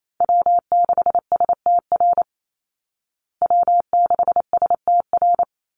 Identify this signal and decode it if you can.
Morse code, heard as a keyed tone, about 28 wpm, 700 Hz. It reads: W6STR W6STR
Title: Call Signs Repeated 3x - 28wpm